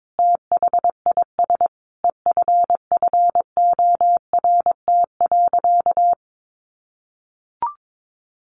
Morse code, heard as a keyed tone, about 22 wpm, 700 Hz. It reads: THIS EFFORT.
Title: Sentences from Top 1400 Words 22wpm